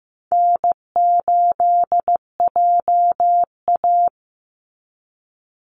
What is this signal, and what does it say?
Morse code, heard as a keyed tone, about 15 wpm, 700 Hz. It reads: N8JA
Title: Easy Call Signs Repeated 3x - 15wpm